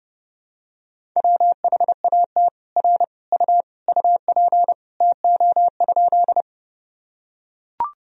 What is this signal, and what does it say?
Morse code, heard as a keyed tone, about 30 wpm, 700 Hz. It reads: WHAT R U UP TO?
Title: Sentences from Top 100 Words and Abbreviations 30wpm